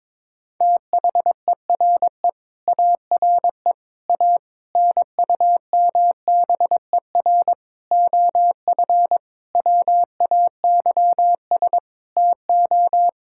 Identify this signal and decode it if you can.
Morse code, heard as a keyed tone, about 22 wpm, 700 Hz. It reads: THERE ARE A NUMBER OF WAYS TO
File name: Sentences with 7-Word Phrases - 22wpm